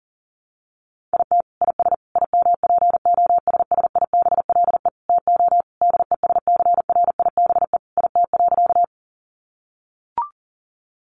Morse code, heard as a keyed tone, about 37 wpm, 700 Hz. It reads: IT IS IMPOSSIBLE TO DESCRIBE IT.
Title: Sentences from Top 1100 Words 40wpm